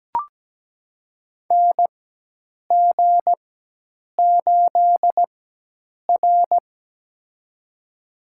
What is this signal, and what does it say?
Morse code, heard as a keyed tone, about 17 wpm, spaced as for 8 wpm, 700 Hz. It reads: NG8R
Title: USA Call Signs - 4x Character Spacing - 17wpm